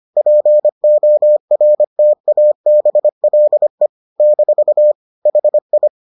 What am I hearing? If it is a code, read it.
Morse code, heard as a keyed tone, about 25 wpm, 600 Hz. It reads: PORTABLE - HI